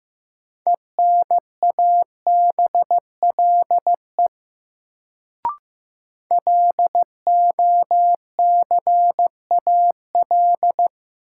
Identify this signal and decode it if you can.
Morse code, heard as a keyed tone, about 15 wpm, 700 Hz. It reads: ENABLE LOCAL
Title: Sentences from Top 1500 Words - New Word Drill Alt - 15wpm